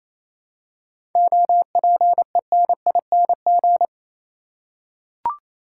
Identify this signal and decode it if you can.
Morse code, heard as a keyed tone, about 28 wpm, 700 Hz. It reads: OPENING
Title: Seven Letter Words - 28wpm